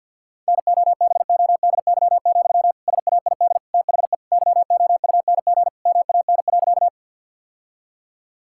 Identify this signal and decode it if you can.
Morse code, heard as a keyed tone, about 51 wpm, 700 Hz. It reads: NOBODY, SAID THE YOUNG MAN.
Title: Sentences from Top 1600 Words 50wpm